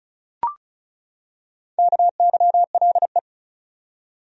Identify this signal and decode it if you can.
Morse code, heard as a keyed tone, about 34 wpm, 700 Hz. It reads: KYLE
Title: USA Common Names - 35wpm